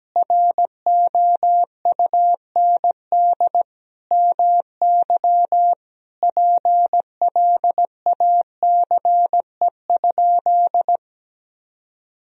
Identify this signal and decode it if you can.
Morse code, heard as a keyed tone, about 17 wpm, 700 Hz. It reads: ROUND MY PLACE?
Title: Sentences from Top 200 Words and Abbreviations 17wpm